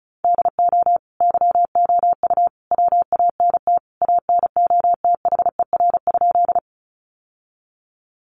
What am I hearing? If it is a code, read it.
Morse code, heard as a keyed tone, about 34 wpm, 700 Hz. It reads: DO YOU WANT ANOTHER?
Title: Sentences from Top 600 Words 35wpm